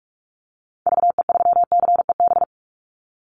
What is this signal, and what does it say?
Morse code, heard as a keyed tone, about 44 wpm, 700 Hz. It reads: VE3CEB
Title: Call Signs Repeated 2x - 45wpm